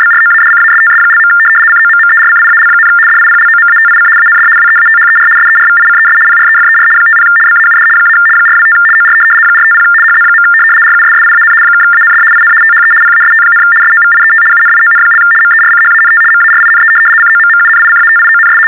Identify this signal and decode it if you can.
RTTY, about 75 baud. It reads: BROADCASTS). COMMERCIAL, DIPLOMATIC AND WEATHER SERVICES PREFER 50 BAUD - 425 OR 450 HZ. RUSSIAN (AND IN THE PAST, SOVIET UNION) MERCHANT MARINE COMMUNICATIONS
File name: teletype-RTTY-75